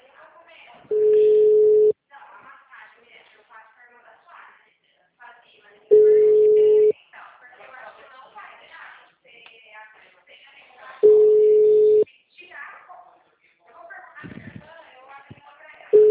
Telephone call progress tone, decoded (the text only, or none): ringback tone